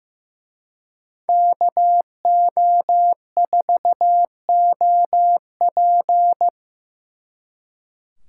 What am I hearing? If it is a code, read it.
Morse code, heard as a keyed tone, about 18 wpm, 700 Hz. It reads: KO4OP
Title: Speed Racing - USA 2x2 Call Signs 15wpm